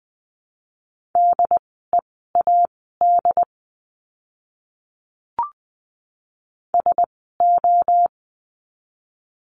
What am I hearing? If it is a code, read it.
Morse code, heard as a keyed tone, about 20 wpm, spaced as for 14 wpm, 700 Hz. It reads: DEAD SO
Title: 075 Lesson 13f - Letter L - Words 20wpm